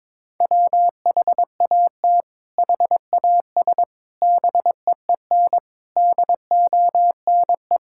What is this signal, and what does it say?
Morse code, heard as a keyed tone, about 22 wpm, 700 Hz. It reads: WHAT HAS BEEN DONE
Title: Sentences from Top 1600 Words 22wpm